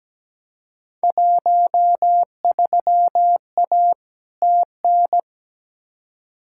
Morse code, heard as a keyed tone, about 17 wpm, 700 Hz. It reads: 13A TN
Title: Field Day Exchanges Encore - 17wpm